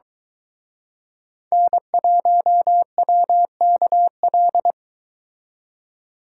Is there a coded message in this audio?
Yes: Morse code from 1.5 to 4.7 s, heard as a keyed tone, about 23 wpm, 700 Hz. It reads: N1WKL